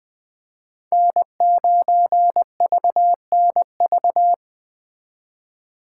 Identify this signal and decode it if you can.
Morse code, heard as a keyed tone, about 21 wpm, 700 Hz. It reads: N9VNV